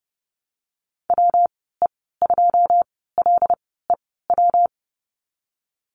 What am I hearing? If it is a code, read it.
Morse code, heard as a keyed tone, about 30 wpm, 700 Hz. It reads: W E 2 L E W
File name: Lesson 19g - Letter w - Call Signs 30wpm